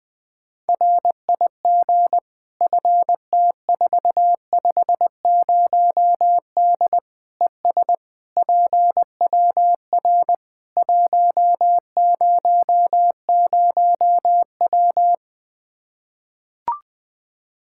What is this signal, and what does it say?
Morse code, heard as a keyed tone, about 21 wpm, 700 Hz. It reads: RIG FT450D ES PWR 100W
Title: Practice QSO - Exchange 2 - Info - 20wpm